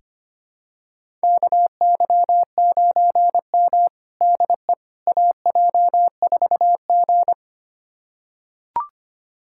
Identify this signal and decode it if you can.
Morse code, heard as a keyed tone, about 25 wpm, 700 Hz. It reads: KY9M DE AJ4G
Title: USA Callsign DE Callsign - 4 Characters - 25wpm